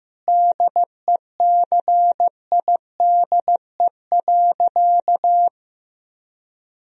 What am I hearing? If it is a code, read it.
Morse code, heard as a keyed tone, about 15 wpm, 700 Hz. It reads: DECIDE.
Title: Sentences from Top 1000 Words - Review Part 3 - 15wpm